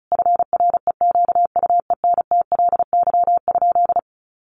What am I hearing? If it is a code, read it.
Morse code, heard as a keyed tone, about 34 wpm, 700 Hz. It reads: FREQUENTLY?
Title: Sentences from Top 1300 Words 35wpm